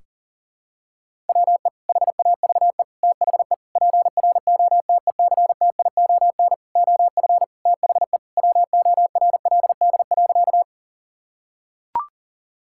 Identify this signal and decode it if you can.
Morse code, heard as a keyed tone, about 40 wpm, 700 Hz. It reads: WE HAVE THE PROTECTION OF THE WORLD.
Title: Sentences from Top 1600 Words 40wpm